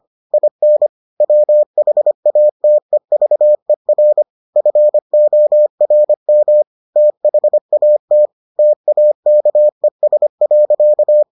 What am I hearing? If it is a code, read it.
Morse code, heard as a keyed tone, about 25 wpm, 600 Hz. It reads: IN WHATEVER FORM THAT TAKES.